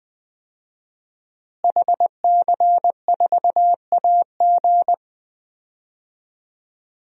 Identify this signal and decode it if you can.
Morse code, heard as a keyed tone, about 20 wpm, 700 Hz. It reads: HC4AG
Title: DXCC-100 Call Signs - 5 Characters - Repeated 2x - 20wpm